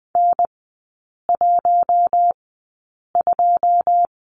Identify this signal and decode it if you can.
Morse code, heard as a keyed tone, about 20 wpm, spaced as for 8 wpm, 690 Hz. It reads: N12
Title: 198 Lesson 32e - Letter k - Three Characters 20wpm